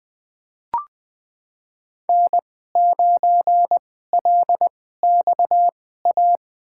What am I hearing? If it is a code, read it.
Morse code, heard as a keyed tone, about 20 wpm, spaced as for 14 wpm, 700 Hz. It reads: N9LXA
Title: USA Call Signs - 5 Characters - 2x Character Spacing - 20wpm